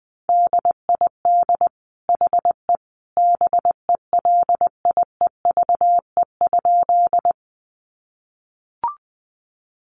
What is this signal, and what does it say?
Morse code, heard as a keyed tone, about 20 wpm, 700 Hz. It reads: DID HE BELIEVE?
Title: Sentences from Top 1000 Words 20wpm